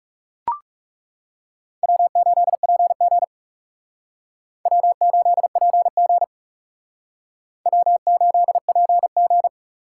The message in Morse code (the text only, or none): W8PG W8PG W8PG